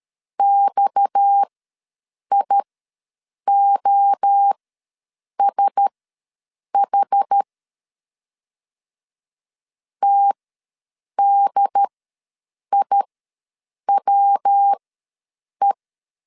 Morse code, heard as a keyed tone, about 13 wpm, spaced as for 7 wpm, 800 Hz. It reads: XIOSH TDIWE
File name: LEZIONI / 7 GF9Z / 7 GF9 ws 12wpm 3344